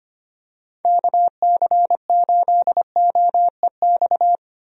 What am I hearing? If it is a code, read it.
Morse code, heard as a keyed tone, about 25 wpm, 700 Hz. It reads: KC8OEX
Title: USA Call Signs - 6 Characters - 25wpm